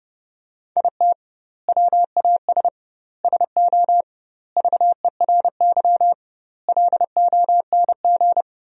Morse code, heard as a keyed tone, about 32 wpm, 700 Hz. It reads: IT WAS SO VERY LONG
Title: 2x Word-Spacing - Sentences from Top 500 Words - 30wpm